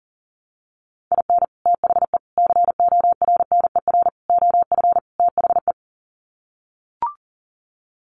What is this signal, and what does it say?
Morse code, heard as a keyed tone, about 40 wpm, 700 Hz. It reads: IN THE CORNER OF THE
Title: Sets of 5 Words Encore 40wpm